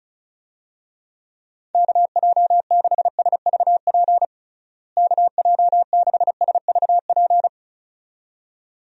Morse code, heard as a keyed tone, about 36 wpm, 700 Hz. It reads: KJ6SVP KJ6SVP
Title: USA Call Signs - 6 Characters - Repeated 2x - 35wpm